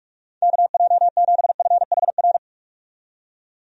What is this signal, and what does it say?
Morse code, heard as a keyed tone, about 43 wpm, 700 Hz. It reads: KJ7FHR
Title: USA Call Signs - 6 Characters - 45wpm